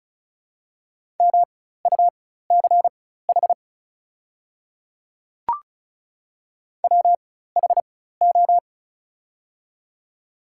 Morse code, heard as a keyed tone, about 34 wpm, spaced as for 16 wpm, 700 Hz. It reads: MUCH WHO
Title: 4x Character Spacing - Top 300 Words - 35wpm